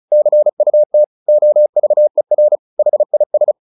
Morse code, heard as a keyed tone, about 34 wpm, 600 Hz. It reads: CUT OVER HIS
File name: the old man and the sea 2 35wpm